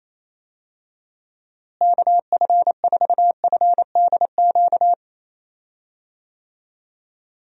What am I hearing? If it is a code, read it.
Morse code, heard as a keyed tone, about 28 wpm, 700 Hz. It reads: KF4FDQ
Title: USA Call Signs - 6 Characters - Repeated 3x - 28wpm